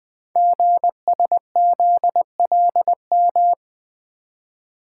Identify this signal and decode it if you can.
Morse code, heard as a keyed tone, about 20 wpm, 700 Hz. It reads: GSZLM
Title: Code Groups - 5 Letters - 20wpm